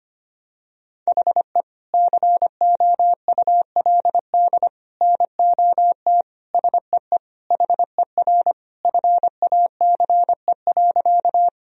Morse code, heard as a keyed tone, about 25 wpm, 700 Hz. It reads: HE COULD NOT SEE HER FACE.
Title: Sentences from Top 1000 Words - Review Part 2 - 25wpm